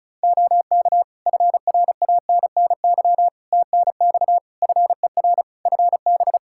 Morse code, heard as a keyed tone, about 36 wpm, 700 Hz. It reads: OK FRANNY TNX FER FB
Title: Practice QSO - Exchange 4 - Ending - 35wpm